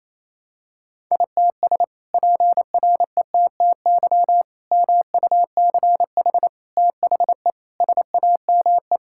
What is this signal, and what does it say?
Morse code, heard as a keyed tone, about 28 wpm, 700 Hz. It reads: ITS PRETTY MUCH THE SAME
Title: Sentences from Top 1000 Words - Review Part 6 - 28wpm